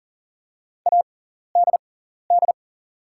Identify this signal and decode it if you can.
Morse code, heard as a keyed tone, about 40 wpm, spaced as for 14 wpm, 690 Hz. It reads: ADD
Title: 6x Character Spacing - Top 300 Words - 40wpm